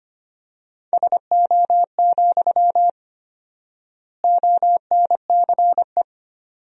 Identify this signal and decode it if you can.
Morse code, heard as a keyed tone, about 25 wpm, 700 Hz. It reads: SO, ONCE
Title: 4x Word-Spacing - Sentences from Top 300 Words - 25wpm